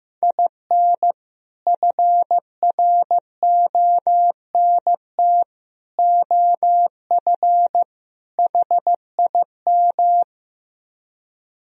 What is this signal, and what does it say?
Morse code, heard as a keyed tone, about 15 wpm, 700 Hz. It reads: IN FRONT OF HIM
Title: Speed-Racing - Sets of 4 Words 15wpm